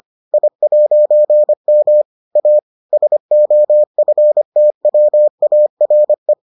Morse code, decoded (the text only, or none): I'M A SOFTWARE